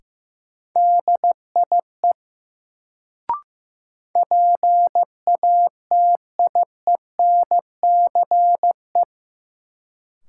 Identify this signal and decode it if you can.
Morse code, heard as a keyed tone, about 15 wpm, 700 Hz. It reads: DIE PATIENCE
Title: Sentences from Top 1300 Words - New Word Drill Alt - 15wpm